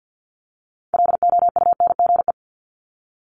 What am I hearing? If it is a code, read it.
Morse code, heard as a keyed tone, about 50 wpm, 690 Hz. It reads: LOUNGE
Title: Multi-letter Phonemes 50wpm